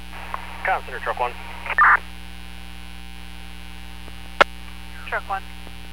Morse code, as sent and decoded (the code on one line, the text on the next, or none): none